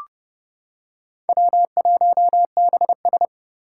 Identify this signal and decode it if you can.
Morse code, heard as a keyed tone, about 30 wpm, 700 Hz. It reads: W1BS
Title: Four Challenging Letter-Number Combinations 30wpm